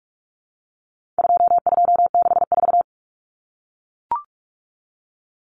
Morse code, heard as a keyed tone, about 44 wpm, 700 Hz. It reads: 2264
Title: Numbers - 4-Digit 45wpm